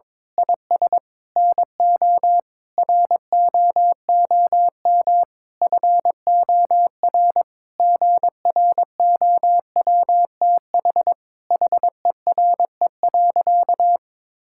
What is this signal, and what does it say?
Morse code, heard as a keyed tone, about 22 wpm, 700 Hz. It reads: IS NO ROOM FOR GROWTH HERE.
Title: Sentences from Top 1000 Words - Review Part 5 - 22wpm